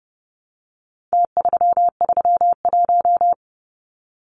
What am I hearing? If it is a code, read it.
Morse code, heard as a keyed tone, about 30 wpm, 700 Hz. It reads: T331
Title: Numbers - Contest Style - 4-Digit 30wpm